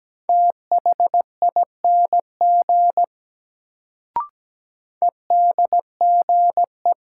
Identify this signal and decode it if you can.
Morse code, heard as a keyed tone, about 17 wpm, 700 Hz. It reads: THING EDGE
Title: Sets of 3 Words - New Word Drill - 17wpm